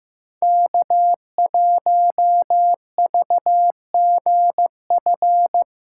Morse code, heard as a keyed tone, about 15 wpm, 700 Hz. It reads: K1VGF
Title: USA Callsign DE Callsign - 5 Characters - 15wpm